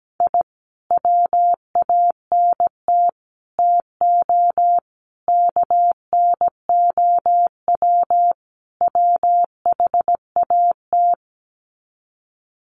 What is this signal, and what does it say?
Morse code, heard as a keyed tone, about 17 wpm, 700 Hz. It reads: I WANT TO KNOW WHAT
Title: Sets of 5 Words Encore2 17wpm